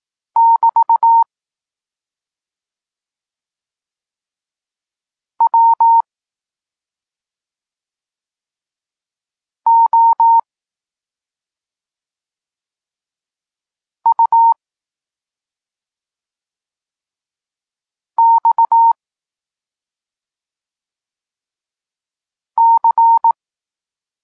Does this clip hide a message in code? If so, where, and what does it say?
Yes: Morse code from 0.4 to 23.3 s, heard as a keyed tone, about 18 wpm, 920 Hz. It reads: <BT> W O U X C